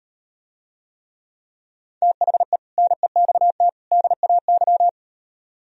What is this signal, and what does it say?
Morse code, heard as a keyed tone, about 38 wpm, 700 Hz. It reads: THE NEXT DAY